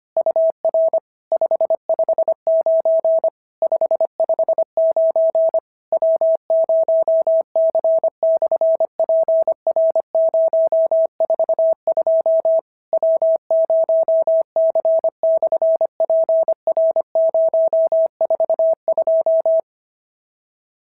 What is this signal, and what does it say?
Morse code, heard as a keyed tone, about 25 wpm, 650 Hz. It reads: UR 559 559 W0C/PR042 W0C/PR042
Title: SOTA QSO Practice 25wpm